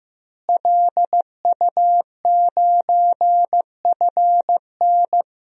Morse code, heard as a keyed tone, about 15 wpm, 700 Hz. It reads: LU9FN
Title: DXCC-100 Call Signs - 5 Characters - Repeated 3x - 15wpm